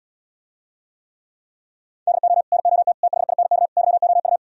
Morse code, heard as a keyed tone, about 77 wpm, 700 Hz. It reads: DO NOT THINK YOU